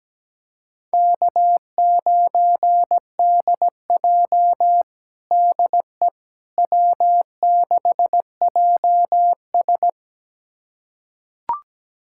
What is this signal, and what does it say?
Morse code, heard as a keyed tone, about 17 wpm, 700 Hz. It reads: K9DJ DE W6JS
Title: USA Callsign DE Callsign - 4 Characters - 17wpm